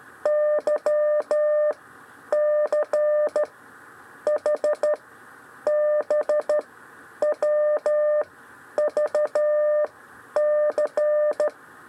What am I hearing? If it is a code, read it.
Morse code, heard as a keyed tone, about 13 wpm, 600 Hz. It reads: Y C H B W V C
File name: 6 lezione cw 1